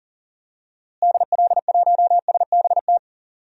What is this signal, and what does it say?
Morse code, heard as a keyed tone, about 40 wpm, 700 Hz. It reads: DL1SBT
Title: DXCC-100 Call Signs - 6 Characters - Repeated 3x - 40wpm